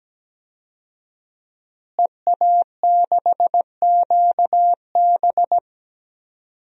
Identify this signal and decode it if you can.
Morse code, heard as a keyed tone, about 17 wpm, 700 Hz. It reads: EA6QB
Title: DXCC-100 Call Signs - 5 Characters - Repeated 2x - 17wpm